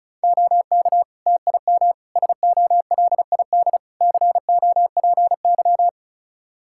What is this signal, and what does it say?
Morse code, heard as a keyed tone, about 34 wpm, 700 Hz. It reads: OK TIM SOLID COPY
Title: Practice QSO - Exchange 3 - More Info - 35wpm